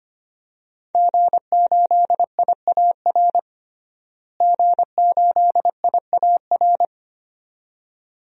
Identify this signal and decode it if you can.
Morse code, heard as a keyed tone, about 25 wpm, 700 Hz. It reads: G8IAR G8IAR
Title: DXCC-100 Call Signs - 5 Characters - Repeated 2x - 25wpm